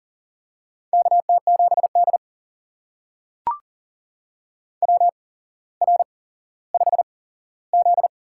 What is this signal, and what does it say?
Morse code, heard as a keyed tone, about 40 wpm, spaced as for 17 wpm, 700 Hz. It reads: KT7D WR5Z
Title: USA Call Signs - 4 Characters - 8x Character Spacing - 40wpm